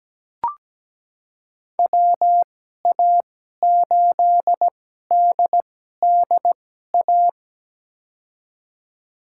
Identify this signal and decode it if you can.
Morse code, heard as a keyed tone, about 17 wpm, spaced as for 12 wpm, 700 Hz. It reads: WA8DDA
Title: USA Call Signs - 2x Character Spacing - 17wpm